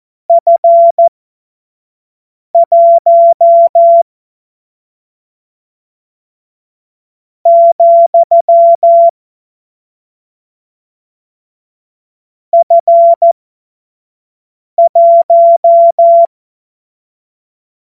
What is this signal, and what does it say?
Morse code, heard as a keyed tone, about 14 wpm, spaced as for 5 wpm, 680 Hz. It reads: F1 , F1
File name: Morsekurs / Witze / Witz 21. Dezember 2021, 25 F70